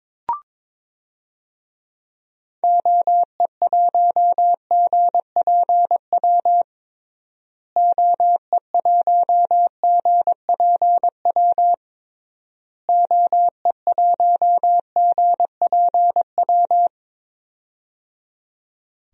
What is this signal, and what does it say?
Morse code, heard as a keyed tone, about 22 wpm, 700 Hz. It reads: OE1GPW OE1GPW OE1GPW
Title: DXCC-100 Call Signs - 6 Characters - Repeated 3x - 22wpm